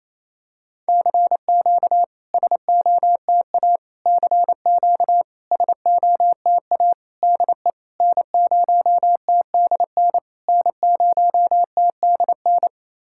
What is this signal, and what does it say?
Morse code, heard as a keyed tone, about 28 wpm, 690 Hz. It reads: CQ SOTA CQ SOTA DE N0TDN N0TDN
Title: SOTA QSO Practice 28wpm